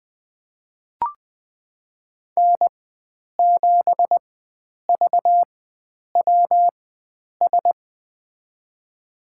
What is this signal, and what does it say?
Morse code, heard as a keyed tone, about 20 wpm, spaced as for 9 wpm, 700 Hz. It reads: N7VWS
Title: USA Call Signs - 4x Character Spacing - 20wpm